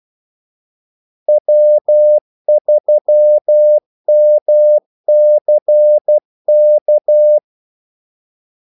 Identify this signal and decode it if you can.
Morse code, heard as a keyed tone, about 12 wpm, 600 Hz. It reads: W3MCK
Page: 2 ekstra ordmellemrum mellem kaldesignalerne